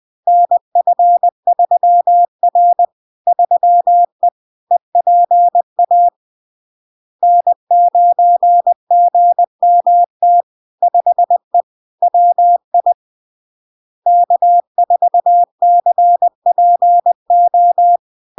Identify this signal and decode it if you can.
Morse code, heard as a keyed tone, about 20 wpm, 700 Hz. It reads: NF3R 3E EPA N9GMT 5E WI K4CPO
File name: LARK-FD-2022-Log-in-CW-20wpm-700-Hz